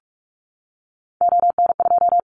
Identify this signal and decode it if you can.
Morse code, heard as a keyed tone, about 44 wpm, 700 Hz. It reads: ON2
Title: Callsign Prefixes for Top 50 DXCC - 45wpm